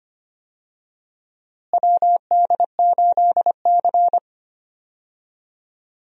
Morse code, heard as a keyed tone, about 25 wpm, 700 Hz. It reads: WD8C
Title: USA Call Signs - 4 Characters - Repeated 2x - 25wpm